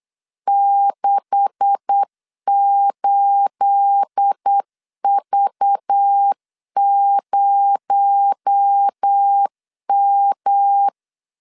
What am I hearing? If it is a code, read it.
Morse code, heard as a keyed tone, about 8 wpm, 800 Hz. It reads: 68V0M
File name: LEZIONI / 10 SIGNS / 10A SLASH 8wpm 3452